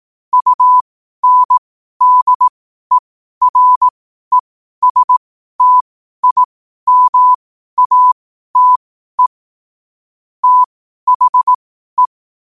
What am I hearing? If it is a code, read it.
Morse code, heard as a keyed tone, about 18 wpm, spaced as for 12 wpm, 1000 Hz. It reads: UNDERESTIMATE THE
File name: qotdfiles / 12wpm